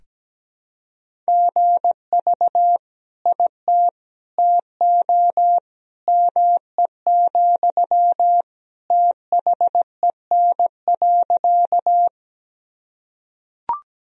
Morse code, heard as a keyed tone, about 17 wpm, 700 Hz. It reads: GV IT TO ME, THEN.
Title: Sentences from Top 200 Words and Abbreviations 17wpm